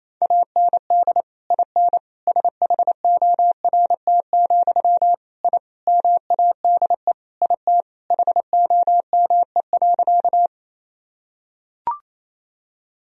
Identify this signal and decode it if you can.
Morse code, heard as a keyed tone, about 28 wpm, 700 Hz. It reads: AND IN SHORT, I MADE IT HOME.
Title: Sentences from Top 1000 Words - Review Part 2 - 28wpm